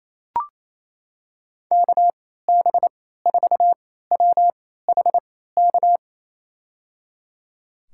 Morse code, heard as a keyed tone, about 28 wpm, spaced as for 16 wpm, 700 Hz. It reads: KB4WHK
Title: USA Call Signs - 6 Characters - 3x Character Spacing - 28wpm